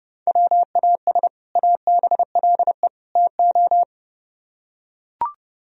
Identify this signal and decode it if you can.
Morse code, heard as a keyed tone, about 29 wpm, 700 Hz. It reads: WAS ABLE TO